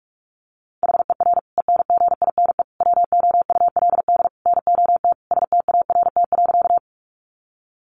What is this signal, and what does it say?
Morse code, heard as a keyed tone, about 44 wpm, 700 Hz. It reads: HER ENGINE WOULD NOT START.
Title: Sentences from Top 1500 Words 45wpm